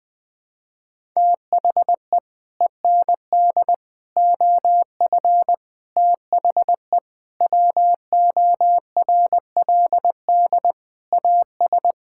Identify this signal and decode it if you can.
Morse code, heard as a keyed tone, about 20 wpm, 700 Hz. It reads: THE END OF THE WORLD AS
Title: Sets of 9 Words - 20wpm